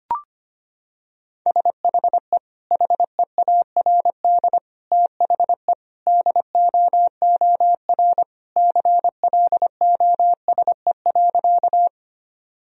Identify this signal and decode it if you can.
Morse code, heard as a keyed tone, about 25 wpm, 700 Hz. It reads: SHE HEARD THE DOOR CLOSE.
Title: Sentences from Top 1000 Words - Review Part 1 - 25wpm